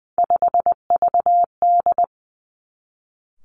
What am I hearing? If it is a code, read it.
Morse code, heard as a keyed tone, about 20 wpm, 700 Hz. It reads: 5VD
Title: Three Challenging Letter-Number Combinations 20wpm